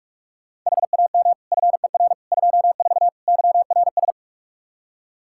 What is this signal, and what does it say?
Morse code, heard as a keyed tone, about 46 wpm, 700 Hz. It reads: HAM FER 24 YRS
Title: Practice QSO - Exchange 3 - More Info - 45wpm